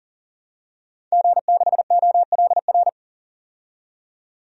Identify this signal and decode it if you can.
Morse code, heard as a keyed tone, about 38 wpm, 700 Hz. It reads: G6OLR
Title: DXCC-100 Call Signs - 5 Characters - 40wpm